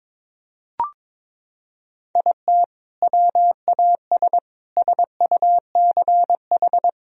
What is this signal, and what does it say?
Morse code, heard as a keyed tone, about 22 wpm, 700 Hz. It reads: IT WAS SUCH
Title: Sentences from Top 1000 Words - Review Part 4 - 22wpm